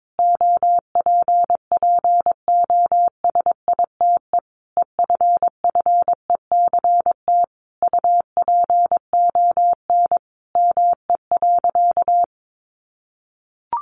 Morse code, heard as a keyed tone, about 22 wpm, 700 Hz. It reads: OPPOSITE EFFECT UPON ME.
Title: Sentences from Top 1300 Words 22wpm